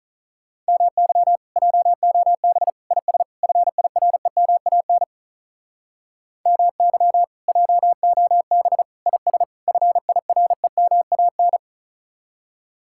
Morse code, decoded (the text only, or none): MY JOB IS FIREMAN MY JOB IS FIREMAN